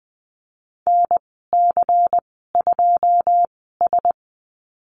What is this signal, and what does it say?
Morse code, heard as a keyed tone, about 20 wpm, spaced as for 15 wpm, 700 Hz. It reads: NC2S
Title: 140 Lesson 23g - Letter f - Call Signs 20wpm